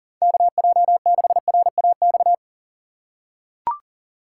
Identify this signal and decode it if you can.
Morse code, heard as a keyed tone, about 40 wpm, 700 Hz. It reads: KJ6RAX
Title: USA Call Signs - 6 Characters - 2x Character Spacing - 40wpm